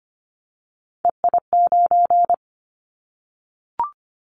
Morse code, heard as a keyed tone, about 25 wpm, 700 Hz. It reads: EI9
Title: Callsign Prefixes for Top 50 DXCC - 25wpm